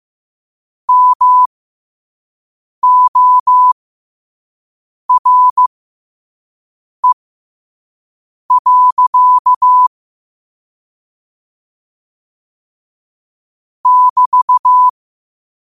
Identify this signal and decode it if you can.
Morse code, heard as a keyed tone, about 15 wpm, spaced as for 5 wpm, 1000 Hz. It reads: MORE. <BT>
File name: qotdfiles / 5wpm